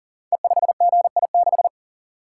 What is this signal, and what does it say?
Morse code, heard as a keyed tone, about 39 wpm, 700 Hz. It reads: E5GI6
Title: Code Groups - 5 Letter:Numbers - 40wpm